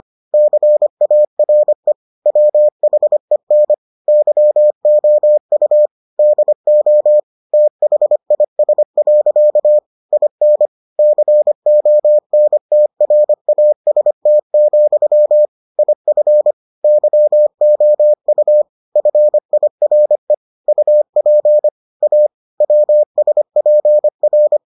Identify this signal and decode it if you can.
Morse code, heard as a keyed tone, about 25 wpm, 600 Hz. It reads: CARE WHEN YOU DO THIS. IN CONTRAST, IF YOU FIRE UP A WSPR